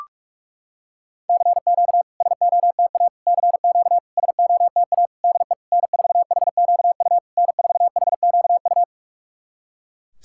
Morse code, heard as a keyed tone, about 46 wpm, 700 Hz. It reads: CQ SOTA CQ SOTA DE N4HQU N4HQU
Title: SOTA QSO Practice 45wpm